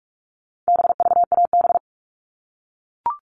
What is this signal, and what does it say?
Morse code, heard as a keyed tone, about 44 wpm, 700 Hz. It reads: BVAB
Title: All Cut Numbers - 4-Digit 45wpm